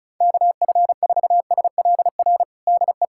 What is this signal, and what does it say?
Morse code, heard as a keyed tone, about 36 wpm, 700 Hz. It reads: KF4SLR DE
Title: USA Callsign DE Callsign - 6 Characters - 35wpm